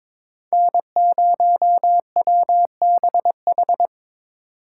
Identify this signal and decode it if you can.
Morse code, heard as a keyed tone, about 25 wpm, 700 Hz. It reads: N0WBH
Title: Speed Racing - USA 1x3 Call Signs 22wpm